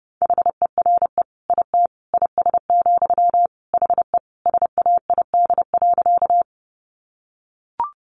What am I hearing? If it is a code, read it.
Morse code, heard as a keyed tone, about 30 wpm, 700 Hz. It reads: HERE IT IS, HE SAID.
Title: Sentences from Top 200 Words 30wpm